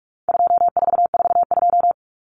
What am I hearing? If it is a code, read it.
Morse code, heard as a keyed tone, about 44 wpm, 700 Hz. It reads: 2442
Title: Numbers - 4-Digit 45wpm